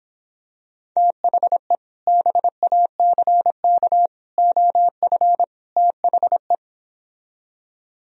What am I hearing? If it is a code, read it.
Morse code, heard as a keyed tone, about 26 wpm, 700 Hz. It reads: THE BACK OF THE